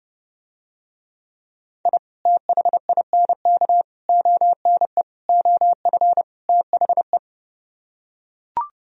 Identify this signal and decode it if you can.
Morse code, heard as a keyed tone, about 32 wpm, 700 Hz. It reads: I THINK ONE OF THE